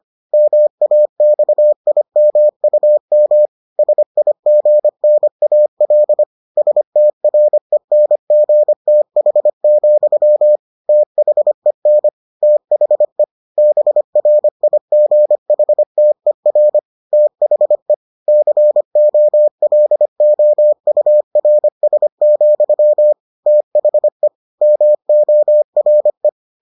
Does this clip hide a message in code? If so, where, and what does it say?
Yes: Morse code from 0.3 to 26.3 s, heard as a keyed tone, about 25 wpm, 600 Hz. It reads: MAXIMUM SIGNAL STRENGTH, THEN THE BRIGHTER THE COLOURS, THE MORE